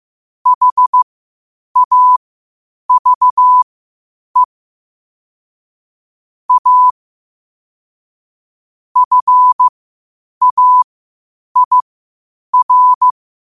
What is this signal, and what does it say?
Morse code, heard as a keyed tone, about 15 wpm, spaced as for 8 wpm, 1000 Hz. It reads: HAVE A FAIR
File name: qotdfiles / 8wpm